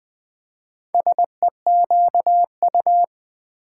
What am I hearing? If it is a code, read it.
Morse code, heard as a keyed tone, about 20 wpm, 700 Hz. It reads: SEQU
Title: Build A Word - 2 Letters At A Time - 8-Letter Words - 20wpm